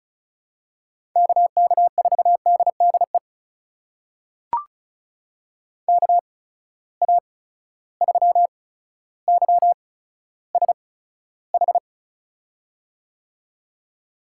Morse code, heard as a keyed tone, about 36 wpm, spaced as for 15 wpm, 700 Hz. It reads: KK4DDE KA3YSH
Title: USA Call Signs - 6 Characters - 8x Character Spacing - 35wpm